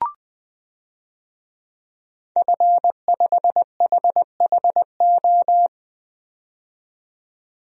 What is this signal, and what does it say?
Morse code, heard as a keyed tone, about 20 wpm, 700 Hz. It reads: F5HHO
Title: DXCC-100 Call Signs - 5 Characters - 20wpm